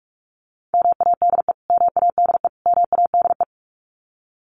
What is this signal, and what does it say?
Morse code, heard as a keyed tone, about 46 wpm, 700 Hz. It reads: MADE MADE MADE
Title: Top 500 Words Repeated 45wpm